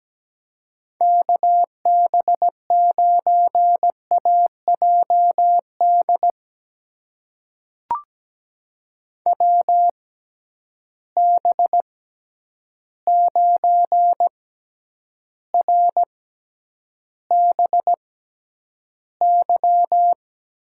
Morse code, heard as a keyed tone, about 17 wpm, spaced as for 9 wpm, 700 Hz. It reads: KB9AJD WB9RBY
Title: USA Call Signs - 6 Characters - 6x Character Spacing - 17wpm